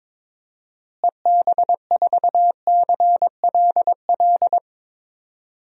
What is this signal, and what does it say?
Morse code, heard as a keyed tone, about 22 wpm, 700 Hz. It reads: EB4CLL
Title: DXCC-100 Call Signs - 6 Characters - Repeated 3x - 22wpm